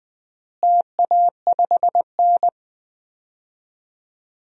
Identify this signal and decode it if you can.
Morse code, heard as a keyed tone, about 20 wpm, 700 Hz. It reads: TA5N
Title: DXCC-100 Call Signs - 4 Characters - Repeated 3x - 20wpm